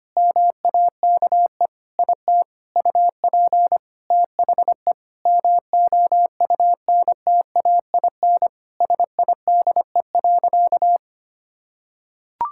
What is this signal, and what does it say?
Morse code, heard as a keyed tone, about 25 wpm, 700 Hz. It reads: MAKE IT UP THE MOUNTAIN SIDE.
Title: Sentences from Top 1000 Words - Review Part 6 - 25wpm